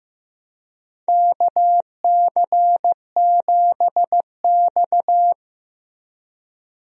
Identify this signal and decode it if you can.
Morse code, heard as a keyed tone, about 15 wpm, 700 Hz. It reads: KC7X
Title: USA Call Signs - 4 Characters - Repeated 3x - 15wpm